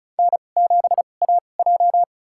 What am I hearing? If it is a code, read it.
Morse code, heard as a keyed tone, about 36 wpm, spaced as for 25 wpm, 700 Hz. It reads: N7AJ
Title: USA Call Signs - 4 Characters - 2x Character Spacing - 35wpm